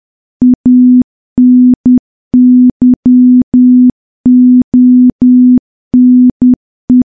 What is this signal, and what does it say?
Morse code, heard as a keyed tone, about 10 wpm, 250 Hz. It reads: ANYONE
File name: morse / morse-training-10WPM